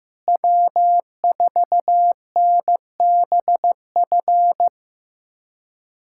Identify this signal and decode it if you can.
Morse code, heard as a keyed tone, about 15 wpm, 700 Hz. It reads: W4NBF
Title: USA Callsign DE Callsign - 5 Characters - 15wpm